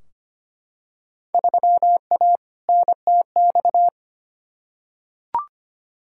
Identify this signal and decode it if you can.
Morse code, heard as a keyed tone, about 25 wpm, 700 Hz. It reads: 3A NTX
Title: Field Day Exchanges - 25wpm